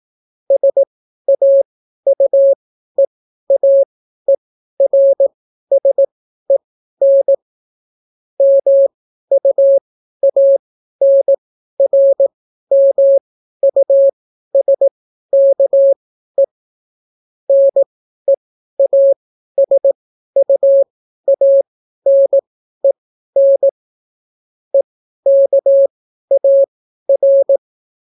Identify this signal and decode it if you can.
Morse code, heard as a keyed tone, about 18 wpm, spaced as for 12 wpm, 550 Hz. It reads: SAUEAERSEN MUANRMUSKE NEASUANEN EKAR